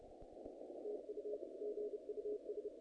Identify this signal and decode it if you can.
Morse code, heard as a keyed tone, about 29 wpm, 420 Hz, with noise. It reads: TU GUS
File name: contests / naqp-cw-k5tr-op-k5pi- -2000